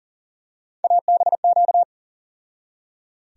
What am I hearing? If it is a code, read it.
Morse code, heard as a keyed tone, about 40 wpm, 700 Hz. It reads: ABQ
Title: USA Call Sign Postfixes - 3 Letters 40wpm